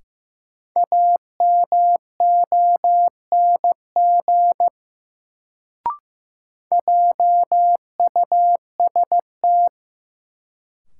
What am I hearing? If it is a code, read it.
Morse code, heard as a keyed tone, about 15 wpm, 700 Hz. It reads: AMONG JUST
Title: Sets of 2 Words - New Word Drill - 15wpm